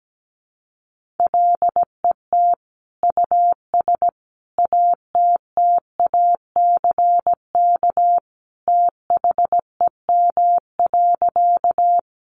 Morse code, heard as a keyed tone, about 17 wpm, 700 Hz. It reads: LET US ATTACK THEM.
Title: Sentences from Top 1500 Words 17wpm